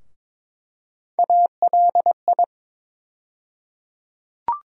Morse code, heard as a keyed tone, about 22 wpm, 700 Hz. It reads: ALI
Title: USA Call Sign Postfixes - 3 Letters 22wpm